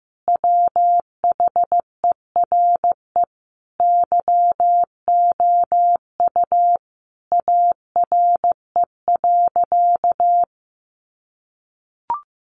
Morse code, heard as a keyed tone, about 15 wpm, 700 Hz. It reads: WHERE YOU ARE.
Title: Sentences from Top 500 Words 15wpm